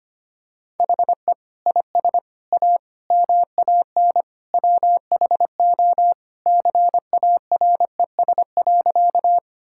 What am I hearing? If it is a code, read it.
Morse code, heard as a keyed tone, about 25 wpm, 700 Hz. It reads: HE IS A MAN WHO CARES.
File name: practice / 25wpm / Sentences with 3-Word Phrases - 25wpm